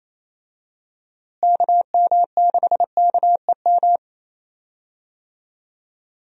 Morse code, heard as a keyed tone, about 28 wpm, 700 Hz. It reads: KM6KEM
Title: USA Call Signs - 6 Characters - Repeated 2x - 28wpm